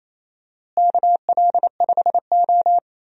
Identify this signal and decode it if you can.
Morse code, heard as a keyed tone, about 28 wpm, 700 Hz. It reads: KL5O
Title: USA Call Signs - 4 Characters - Repeated 3x - 28wpm